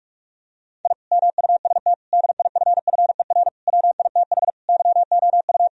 Morse code, heard as a keyed tone, about 46 wpm, 700 Hz. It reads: I MUST DIFFER WITH YOU
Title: Sentences from Top 1000 Words - Review Part 1 - 45wpm